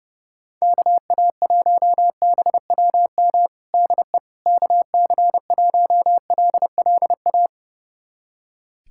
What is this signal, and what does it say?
Morse code, heard as a keyed tone, about 30 wpm, 700 Hz. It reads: KA1BWM DE KC1LLA
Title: USA Callsign DE Callsign 30wpm